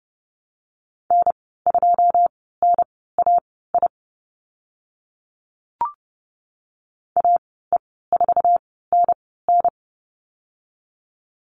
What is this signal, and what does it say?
Morse code, heard as a keyed tone, about 30 wpm, spaced as for 17 wpm, 700 Hz. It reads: N2NAI AE4NN
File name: Lesson 15g - Number 5 - Call Signs 30wpm